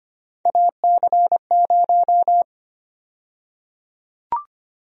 Morse code, heard as a keyed tone, about 25 wpm, 700 Hz. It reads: AC0
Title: USA Call Sign Prefixes - 25wpm